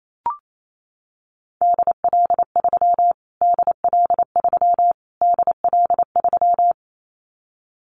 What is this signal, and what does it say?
Morse code, heard as a keyed tone, about 28 wpm, 700 Hz. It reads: DL3 DL3 DL3
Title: Callsign Prefixes for Top 25 DXCC - Repeated 3x - 28wpm